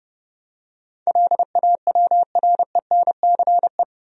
Morse code, heard as a keyed tone, about 30 wpm, 700 Hz. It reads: LAWRENCE
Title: USA Common Names - 30wpm